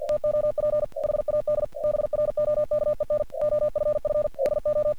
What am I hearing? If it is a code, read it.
Morse code, heard as a keyed tone, about 48 wpm, 600 Hz. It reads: TOP HAD BROKEN OFF SO
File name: archived / CRSnet